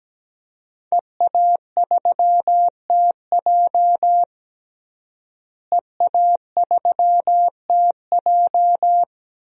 Morse code, heard as a keyed tone, about 17 wpm, 700 Hz. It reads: EA3TJ EA3TJ
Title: DXCC-100 Call Signs - 5 Characters - Repeated 3x - 17wpm